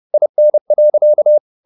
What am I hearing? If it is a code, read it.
Morse code, heard as a keyed tone, about 30 wpm, 600 Hz. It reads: IN.